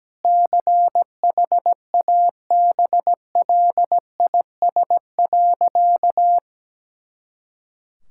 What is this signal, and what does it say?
Morse code, heard as a keyed tone, about 17 wpm, 700 Hz. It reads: CHABLIS.
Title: Pangram Challenge v2 - 17wpm